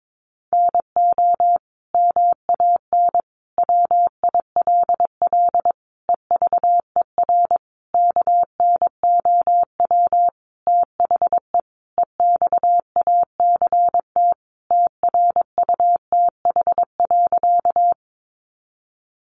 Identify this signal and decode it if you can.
Morse code, heard as a keyed tone, about 22 wpm, 700 Hz. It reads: NO MAN WILL EVER KNOW THE EXACT TRUTH.
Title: Sentences from Top 1300 Words 22wpm